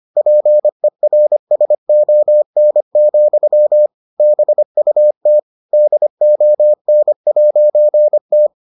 Morse code, heard as a keyed tone, about 25 wpm, 600 Hz. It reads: PERSON, BUT DON'T